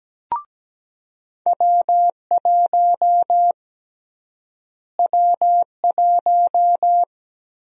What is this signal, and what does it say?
Morse code, heard as a keyed tone, about 17 wpm, 700 Hz. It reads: W1 W1
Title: USA Call Sign Prefixes - Repeated 2x - 17wpm